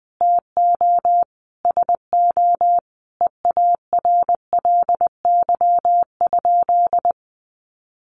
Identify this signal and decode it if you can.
Morse code, heard as a keyed tone, about 20 wpm, 700 Hz. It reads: TO SO EARLY?
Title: Sentences from Top 400 Words 20wpm